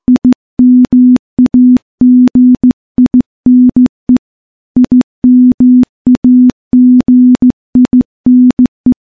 Morse code, heard as a keyed tone, about 15 wpm, 250 Hz. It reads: IMAGINE IMAGINE
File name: morse / morse-training-15WPM